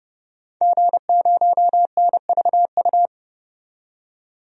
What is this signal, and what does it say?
Morse code, heard as a keyed tone, about 30 wpm, 700 Hz. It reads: G0NVU
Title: DXCC-100 Call Signs - 5 Characters - 30wpm